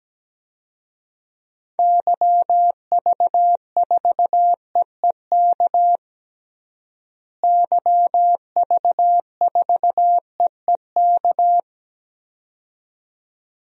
Morse code, heard as a keyed tone, about 17 wpm, 700 Hz. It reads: YV4EEK YV4EEK
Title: DXCC-100 Call Signs - 6 Characters - Repeated 2x - 17wpm